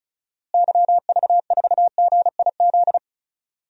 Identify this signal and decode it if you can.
Morse code, heard as a keyed tone, about 36 wpm, 700 Hz. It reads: YV4GIZ
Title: DXCC-100 Call Signs - 6 Characters - Repeated 2x - 35wpm